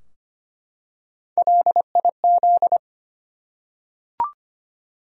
Morse code, heard as a keyed tone, about 25 wpm, 700 Hz. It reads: LIZ
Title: USA Common Names - Max 3 Letters - 25wpm